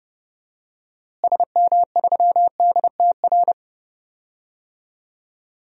Morse code, heard as a keyed tone, about 30 wpm, 700 Hz. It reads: SM3DTR
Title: DXCC-100 Call Signs - 6 Characters - Repeated 3x - 30wpm